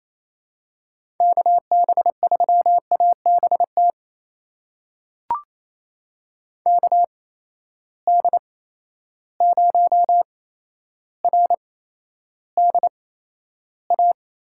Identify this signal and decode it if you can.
Morse code, heard as a keyed tone, about 28 wpm, spaced as for 12 wpm, 700 Hz. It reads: KB3ABT KD0RDA
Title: USA Call Signs - 6 Characters - 8x Character Spacing - 28wpm